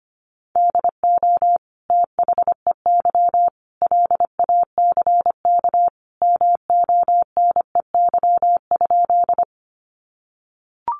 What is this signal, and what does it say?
Morse code, heard as a keyed tone, about 25 wpm, 700 Hz. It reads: DO THEY LACK MONEY?
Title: Sentences from Top 1100 Words 25wpm